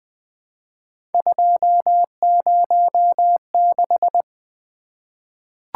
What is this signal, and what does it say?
Morse code, heard as a keyed tone, about 20 wpm, 700 Hz. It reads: 206
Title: Three Letter-Number Combinations - 20wpm